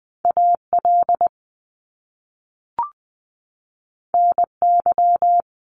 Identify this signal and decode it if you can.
Morse code, heard as a keyed tone, about 20 wpm, 700 Hz. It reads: A L N Y
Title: Top Two Letter Combinations in Words 20wpm